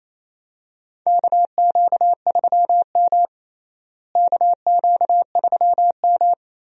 Morse code, heard as a keyed tone, about 28 wpm, 700 Hz. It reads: KQ3M KQ3M
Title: USA Call Signs - 4 Characters - Repeated 3x - 28wpm